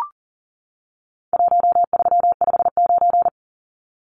Morse code, heard as a keyed tone, about 40 wpm, 700 Hz. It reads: 1359
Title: Numbers - Contest Style - 4-Digit 40wpm